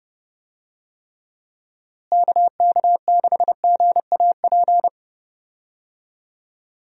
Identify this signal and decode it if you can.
Morse code, heard as a keyed tone, about 30 wpm, 700 Hz. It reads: KK6GAP
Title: USA Call Signs - 6 Characters - Repeated 3x - 30wpm